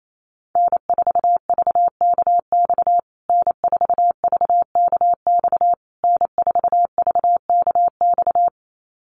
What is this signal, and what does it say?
Morse code, heard as a keyed tone, about 28 wpm, 700 Hz. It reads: N4VKX N4VKX N4VKX
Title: USA Call Signs Repeated 3x 28wpm